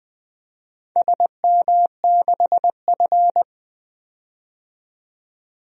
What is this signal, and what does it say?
Morse code, heard as a keyed tone, about 20 wpm, 700 Hz. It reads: SM6F
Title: DXCC-100 Call Signs - 4 Characters - 20wpm